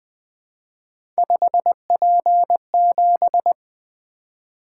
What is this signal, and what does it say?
Morse code, heard as a keyed tone, about 20 wpm, 700 Hz. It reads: 5P7
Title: Three Letter-Number Combinations - 20wpm